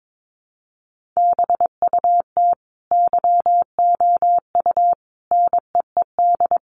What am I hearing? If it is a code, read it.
Morse code, heard as a keyed tone, about 22 wpm, 700 Hz. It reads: BUT YOU NEED
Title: Sentences from Top 1300 Words 22wpm